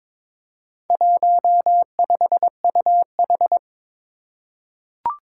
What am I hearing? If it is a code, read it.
Morse code, heard as a keyed tone, about 22 wpm, 700 Hz. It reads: 15UH
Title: Four Challenging Letter-Number Combinations 22wpm